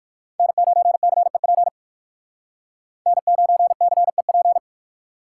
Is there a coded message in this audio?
Yes: Morse code from 0.4 to 4.6 s, heard as a keyed tone, about 57 wpm, 700 Hz. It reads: N9CEP N9CEP